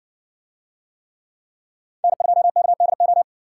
Morse code, heard as a keyed tone, about 60 wpm, 700 Hz. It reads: N2KDO